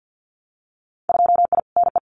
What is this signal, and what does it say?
Morse code, heard as a keyed tone, about 50 wpm, 700 Hz. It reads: 2I NE
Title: Winter Field Day Exchanges Encore 50wpm